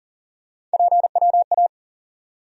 Morse code, heard as a keyed tone, about 40 wpm, 700 Hz. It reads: PWA
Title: USA Call Sign Postfixes - 3 Letters 40wpm